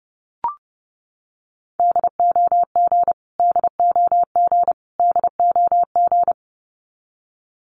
Morse code, heard as a keyed tone, about 30 wpm, 700 Hz. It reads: DOG DOG DOG
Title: Top 500 Words Repeated 30wpm